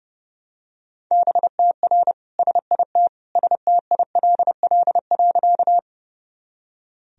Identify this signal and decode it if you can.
Morse code, heard as a keyed tone, about 30 wpm, 700 Hz. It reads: BTR SIT STILL.
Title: Sentences from Top 500 Words and Abbreviations 30wpm